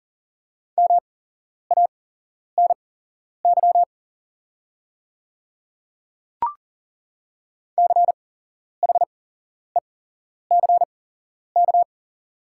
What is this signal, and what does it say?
Morse code, heard as a keyed tone, about 40 wpm, spaced as for 11 wpm, 690 Hz. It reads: MANY CHECK
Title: 8x Character Spacing - Top 500 Words - 40wpm